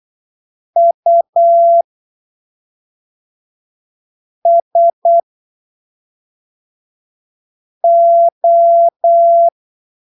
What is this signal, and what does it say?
Morse code, heard as a keyed tone, about 8 wpm, spaced as for 3 wpm, 680 Hz. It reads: USO
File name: arquivos / texto / texto 03 08ppm